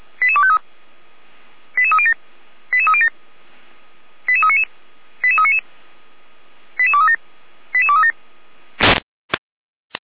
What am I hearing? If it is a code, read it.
Five-tone selective call (ZVEI), heard as quick runs of five tones, tones of about 70 ms. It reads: 88253 88307 88307 88399 88399 88237 88237